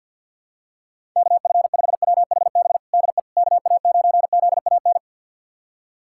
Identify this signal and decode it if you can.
Morse code, heard as a keyed tone, about 46 wpm, 700 Hz. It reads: KF5WHB DE KA9ZAN
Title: USA Callsign DE Callsign - 6 Characters - 50wpm